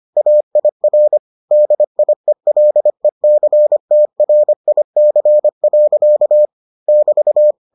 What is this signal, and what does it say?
Morse code, heard as a keyed tone, about 25 wpm, 600 Hz. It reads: AIR DIELECTRIC. <BT>